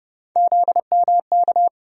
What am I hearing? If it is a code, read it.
Morse code, heard as a keyed tone, about 30 wpm, 700 Hz. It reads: ZMK
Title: USA Call Sign Postfixes - 3 Letters 30wpm